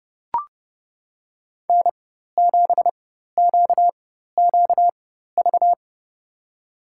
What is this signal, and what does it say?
Morse code, heard as a keyed tone, about 30 wpm, spaced as for 14 wpm, 700 Hz. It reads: N7QQV
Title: USA Call Signs - 4x Character Spacing - 30wpm